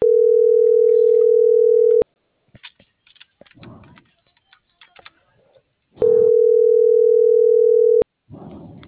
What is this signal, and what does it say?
Ringback tone (North America), 440 + 480 Hz.